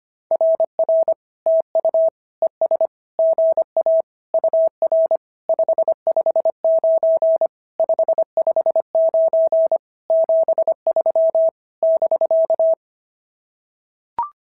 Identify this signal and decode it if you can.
Morse code, heard as a keyed tone, about 24 wpm, 650 Hz. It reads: RR TU ES GA UR 559 559 73 <BK>
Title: SOTA QSO Practice 25wpm